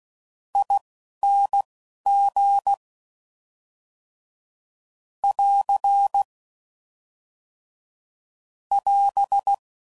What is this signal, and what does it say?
Morse code, heard as a keyed tone, about 16 wpm, spaced as for 12 wpm, 800 Hz. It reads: ING <AR> <AS>